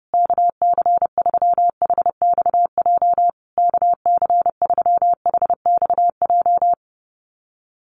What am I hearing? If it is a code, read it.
Morse code, heard as a keyed tone, about 30 wpm, 700 Hz. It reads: KC3HXJ KC3HXJ
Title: USA Call Signs Repeated 3x 30wpm